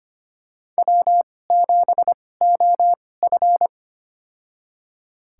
Morse code, heard as a keyed tone, about 25 wpm, spaced as for 18 wpm, 700 Hz. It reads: W7OF
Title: USA Call Signs - 2x Character Spacing - 25wpm